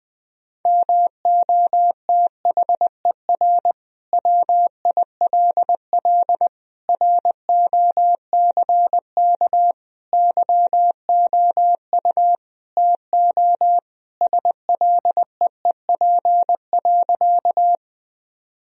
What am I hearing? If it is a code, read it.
Morse code, heard as a keyed tone, about 20 wpm, 700 Hz. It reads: MOTHER WILL ROCK YOU TO SLEEP.
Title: Sentences from Top 1000 Words - Review Part 2 - 20wpm